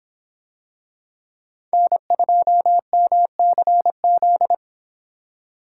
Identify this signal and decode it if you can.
Morse code, heard as a keyed tone, about 26 wpm, 700 Hz. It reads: N2MCZ